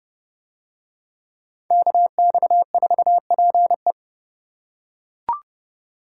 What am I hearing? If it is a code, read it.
Morse code, heard as a keyed tone, about 28 wpm, 700 Hz. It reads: KX4PE